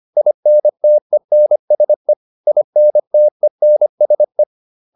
Morse code, heard as a keyed tone, about 25 wpm, 600 Hz. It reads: INTENSE INTENSE